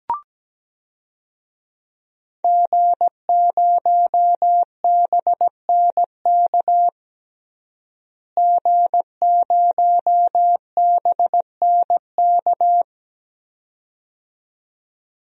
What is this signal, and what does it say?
Morse code, heard as a keyed tone, about 17 wpm, 700 Hz. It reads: G0BNK G0BNK
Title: DXCC-100 Call Signs - 5 Characters - Repeated 2x - 17wpm